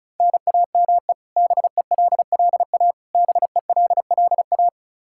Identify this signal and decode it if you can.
Morse code, heard as a keyed tone, about 34 wpm, 700 Hz. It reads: NAME BELLA BELLA
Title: Practice QSO - Exchange 1 - Beginning - 35wpm